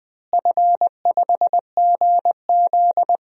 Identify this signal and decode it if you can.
Morse code, heard as a keyed tone, about 21 wpm, 700 Hz. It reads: F5GZ
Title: DXCC-100 Call Signs - 4 Characters - Repeated 2x - 20wpm